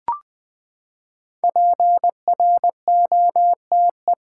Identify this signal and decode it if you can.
Morse code, heard as a keyed tone, about 20 wpm, 700 Hz. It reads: PROTE
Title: Build A Word - 5 Letters At A Time - 10-Letter Words - 20wpm